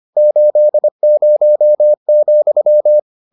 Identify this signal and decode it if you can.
Morse code, heard as a keyed tone, about 25 wpm, 600 Hz. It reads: 80,